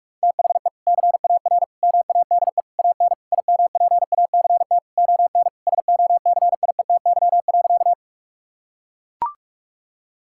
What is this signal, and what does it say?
Morse code, heard as a keyed tone, about 43 wpm, 700 Hz. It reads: THE CAR MADE AN IMPACT ON SOCIETY.
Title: Sentences from Top 1000 Words - Review Part 6 - 45wpm